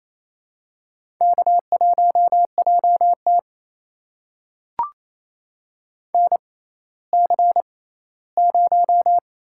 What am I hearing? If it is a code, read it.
Morse code, heard as a keyed tone, about 28 wpm, spaced as for 16 wpm, 700 Hz. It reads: K1JT NC0
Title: USA Call Signs - 4 Characters - 6x Character Spacing - 28wpm